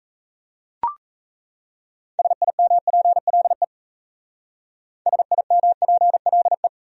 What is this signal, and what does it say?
Morse code, heard as a keyed tone, about 35 wpm, 700 Hz. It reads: SIMPLE SIMPLE